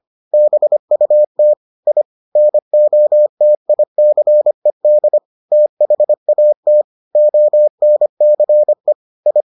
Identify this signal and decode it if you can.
Morse code, heard as a keyed tone, about 25 wpm, 600 Hz. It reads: BUT I NOTICED THAT ONCE I